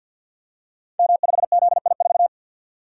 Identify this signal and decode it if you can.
Morse code, heard as a keyed tone, about 51 wpm, 700 Hz. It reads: M5ZI4
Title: Code Groups - 5 Letter:Numbers - 50wpm